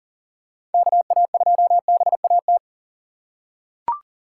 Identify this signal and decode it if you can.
Morse code, heard as a keyed tone, about 40 wpm, 700 Hz. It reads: KA2BAT
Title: USA Call Signs - 6 Characters - 3x Character Spacing - 40wpm